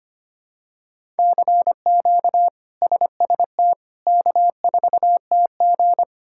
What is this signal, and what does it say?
Morse code, heard as a keyed tone, about 25 wpm, 700 Hz. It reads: CQ SST K4TG
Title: K1USN SST QSO Practice - 25wpm